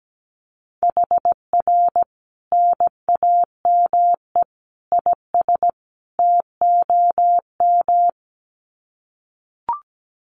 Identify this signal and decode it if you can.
Morse code, heard as a keyed tone, about 17 wpm, 700 Hz. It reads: HR NAME IS TOM
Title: Common QSO Phrases 17wpm